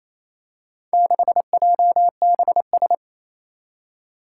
Morse code, heard as a keyed tone, about 28 wpm, 700 Hz. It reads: 6JBS
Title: Four Challenging Letter-Number Combinations 28wpm